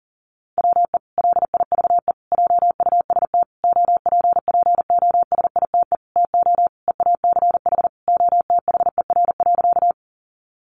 Morse code, heard as a keyed tone, about 37 wpm, 700 Hz. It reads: WE LIVE JUST OPPOSITE TO EACH OTHER.
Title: Sentences from Top 1300 Words 40wpm